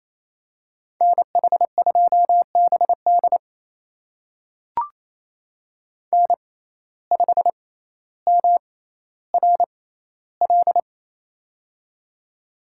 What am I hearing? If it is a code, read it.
Morse code, heard as a keyed tone, about 27 wpm, spaced as for 14 wpm, 700 Hz. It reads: NH2BD N5MRL
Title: USA Call Signs - 5 Characters - 6x Character Spacing - 28wpm